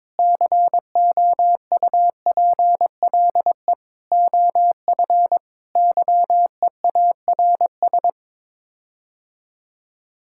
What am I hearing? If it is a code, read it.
Morse code, heard as a keyed tone, about 22 wpm, 700 Hz. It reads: COUPLE OF YEARS
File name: practice / 22wpm / Sets of 6 Words - 22wpm